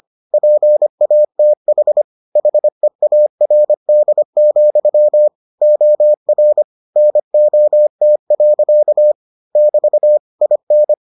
Morse code, heard as a keyed tone, about 25 wpm, 600 Hz. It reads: PATH HEARD, OR NOT. <BT> IN